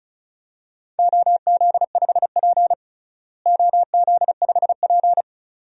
Morse code, heard as a keyed tone, about 34 wpm, 700 Hz. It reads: OZ5P OZ5P
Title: DXCC-100 Call Signs - 4 Characters - Repeated 2x - 35wpm